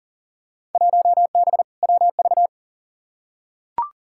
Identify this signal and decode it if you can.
Morse code, heard as a keyed tone, about 40 wpm, 700 Hz. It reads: 1B WV
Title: Field Day Exchanges Encore - 40wpm